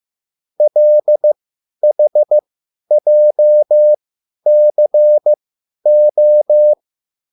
Morse code, heard as a keyed tone, about 15 wpm, spaced as for 10 wpm, 600 Hz. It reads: LHJCO
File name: QSL / CW lesson